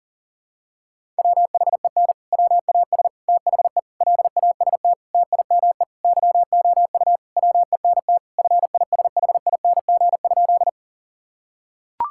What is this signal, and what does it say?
Morse code, heard as a keyed tone, about 40 wpm, 700 Hz. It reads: WHEN WAS THE LAST TIME YOU WENT FISHING?
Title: Sentences from Top 1000 Words - Review Part 5 - 40wpm